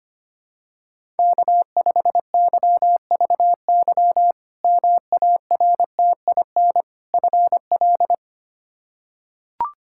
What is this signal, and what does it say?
Morse code, heard as a keyed tone, about 25 wpm, 700 Hz. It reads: K5YVY MARTIN FL
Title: K1USN SST QSO Practice - 25wpm